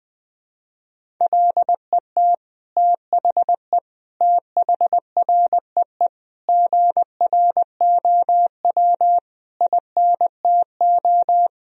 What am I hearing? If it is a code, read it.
Morse code, heard as a keyed tone, about 20 wpm, 700 Hz. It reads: LET THE THREE GROW INTO
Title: Sentences from Top 1000 Words - Review Part 1 - 20wpm